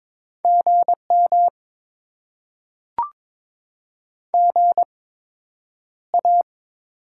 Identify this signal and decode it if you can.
Morse code, heard as a keyed tone, about 22 wpm, spaced as for 9 wpm, 700 Hz. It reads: GM GA
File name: Top 20 QSO Elements - 8x Character Spacing - 22wpm